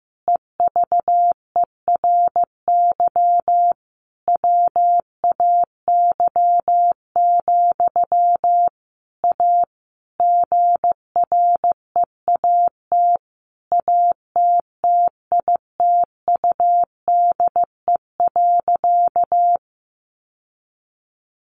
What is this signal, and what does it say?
Morse code, heard as a keyed tone, about 15 wpm, 700 Hz. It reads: EVERY WAY, A GREAT ATTITUDE.
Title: Sentences from Top 1300 Words 15wpm